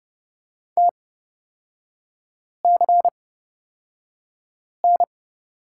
Morse code, heard as a keyed tone, about 31 wpm, spaced as for 5 wpm, 690 Hz. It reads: TCN
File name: Lesson 37e - Letter x - Three Characters 30wpm